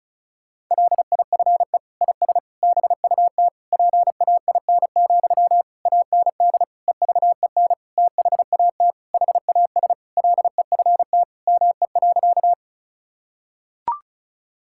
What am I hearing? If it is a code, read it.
Morse code, heard as a keyed tone, about 36 wpm, 700 Hz. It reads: LIFE IS BUT PAIN, AND EVEN THAT HAS LEFT ME.
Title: Sentences from Top 1600 Words 35wpm